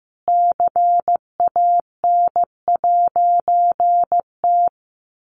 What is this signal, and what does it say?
Morse code, heard as a keyed tone, about 14 wpm, 700 Hz. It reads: CAN'T
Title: A Sound Of Thunder 15wpm